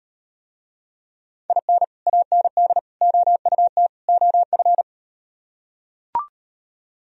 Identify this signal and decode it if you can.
Morse code, heard as a keyed tone, about 39 wpm, 700 Hz. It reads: IN AND OUT OF